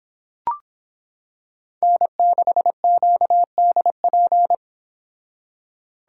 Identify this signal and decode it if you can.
Morse code, heard as a keyed tone, about 26 wpm, 700 Hz. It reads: N6QDP